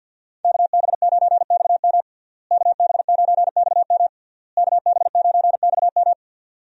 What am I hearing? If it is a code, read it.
Morse code, heard as a keyed tone, about 46 wpm, 700 Hz. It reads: KB9XM KB9XM KB9XM
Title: USA Call Signs - 5 Characters - Repeated 3x - 50wpm